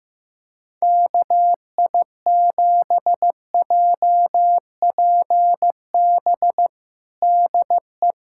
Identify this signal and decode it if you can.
Morse code, heard as a keyed tone, about 15 wpm, 700 Hz. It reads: KI7JPB DE
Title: USA Callsign DE Callsign 15wpm